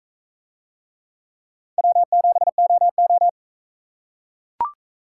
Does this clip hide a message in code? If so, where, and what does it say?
Yes: Morse code from 1.8 to 3.3 s, heard as a keyed tone, about 41 wpm, 700 Hz. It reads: W7OO